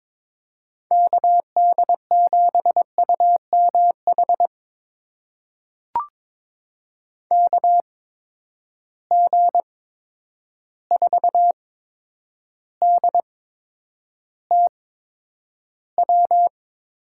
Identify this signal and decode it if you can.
Morse code, heard as a keyed tone, about 21 wpm, spaced as for 9 wpm, 700 Hz. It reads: KD7UMH KG4DTW
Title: USA Call Signs - 6 Characters - 8x Character Spacing - 22wpm